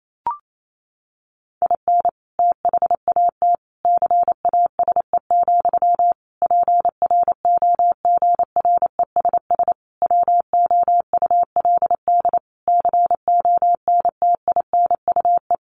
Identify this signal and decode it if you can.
Morse code, heard as a keyed tone, about 28 wpm, 700 Hz. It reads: IN THAT CASE, PROGRESS WOULD CONTINUE
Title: Sentences from Top 1400 Words 28wpm